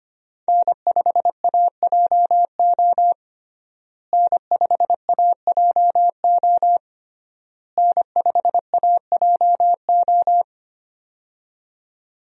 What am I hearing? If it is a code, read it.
Morse code, heard as a keyed tone, about 25 wpm, 700 Hz. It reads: N5AJO N5AJO N5AJO
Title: USA Call Signs - 5 Characters - Repeated 3x - 25wpm